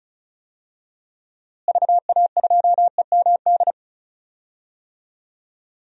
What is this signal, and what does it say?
Morse code, heard as a keyed tone, about 36 wpm, 700 Hz. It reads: VA2EMD
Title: DXCC-100 Call Signs - 6 Characters - Repeated 3x - 35wpm